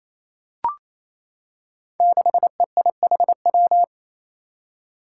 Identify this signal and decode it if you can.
Morse code, heard as a keyed tone, about 28 wpm, 700 Hz. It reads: 6EIHW
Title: Five Challenging Letter-Number Combinations 28wpm